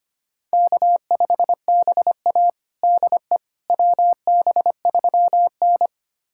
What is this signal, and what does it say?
Morse code, heard as a keyed tone, about 25 wpm, 700 Hz. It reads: K5BA DE WB3N
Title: USA Callsign DE Callsign - 4 Characters - 25wpm